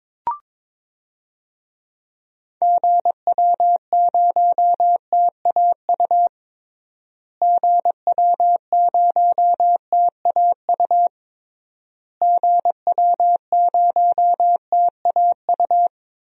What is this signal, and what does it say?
Morse code, heard as a keyed tone, about 22 wpm, 700 Hz. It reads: GW0TAU GW0TAU GW0TAU
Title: DXCC-100 Call Signs - 6 Characters - Repeated 3x - 22wpm